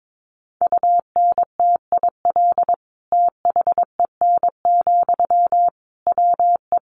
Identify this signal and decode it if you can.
Morse code, heard as a keyed tone, about 22 wpm, 700 Hz. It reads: UNTIL THEN, WE
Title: Sentences from Top 1500 Words 22wpm